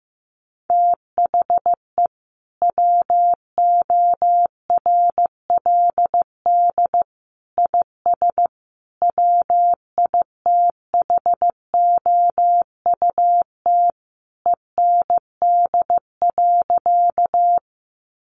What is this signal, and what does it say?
Morse code, heard as a keyed tone, about 15 wpm, 700 Hz. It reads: THE WORLD IS WITHOUT END.
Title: Sentences from Top 600 Words 15wpm